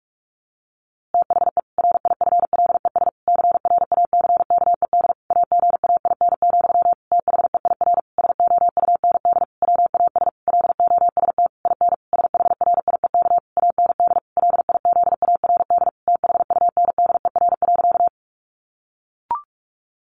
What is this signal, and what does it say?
Morse code, heard as a keyed tone, about 44 wpm, 700 Hz. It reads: THE RIFLES CRACKED AGAIN, THEIR SOUND WAS LOST IN SHRIEK AND LIZARD THUNDER.
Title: A Sound Of Thunder 45wpm